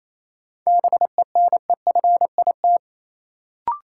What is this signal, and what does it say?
Morse code, heard as a keyed tone, about 28 wpm, 700 Hz. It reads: BENEFIT
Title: Sentences from Top 1000 Words - Review Part 6 - New Word Drill - 28wpm